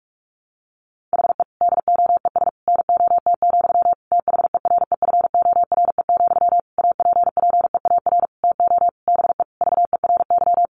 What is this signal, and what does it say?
Morse code, heard as a keyed tone, about 44 wpm, 700 Hz. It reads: HE DOES NOT, THEREFORE, APPEAR TO BE VERY
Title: Sentences from Top 1200 Words 45wpm